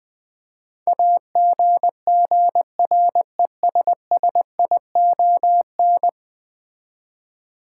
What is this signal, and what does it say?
Morse code, heard as a keyed tone, about 20 wpm, 700 Hz. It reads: AGGRESSION
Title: Build A Word - 5 Letters At A Time - 10-Letter Words - 20wpm